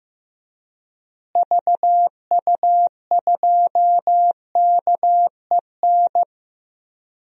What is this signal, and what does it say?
Morse code, heard as a keyed tone, about 16 wpm, 700 Hz. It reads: VU2KEN
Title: DXCC-100 Call Signs - 6 Characters - Repeated 3x - 15wpm